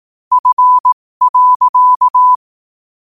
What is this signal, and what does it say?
Morse code, heard as a keyed tone, about 18 wpm, spaced as for 15 wpm, 1000 Hz. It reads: F.
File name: qotdfiles / 15wpm